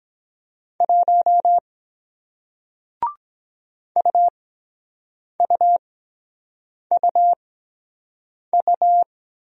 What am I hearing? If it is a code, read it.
Morse code, heard as a keyed tone, about 23 wpm, 700 Hz. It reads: 1 U U U U